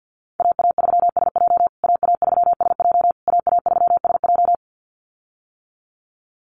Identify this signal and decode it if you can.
Morse code, heard as a keyed tone, about 48 wpm, 700 Hz. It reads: AA3SJ AA3SJ AA3SJ
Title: USA Call Signs Repeated 3x 50wpm